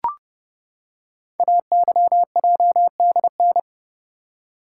Morse code, heard as a keyed tone, about 30 wpm, 700 Hz. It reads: AYJDN
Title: Code Groups - 5 Letters - 30wpm